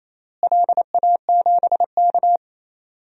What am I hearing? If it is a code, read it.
Morse code, heard as a keyed tone, about 28 wpm, 700 Hz. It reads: LA7K
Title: DXCC-100 Call Signs - 4 Characters - Repeated 2x - 28wpm